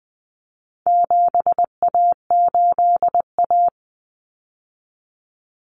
Morse code, heard as a keyed tone, about 20 wpm, 700 Hz. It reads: 7A8A
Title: Common Cut Numbers - 4-Digit 20wpm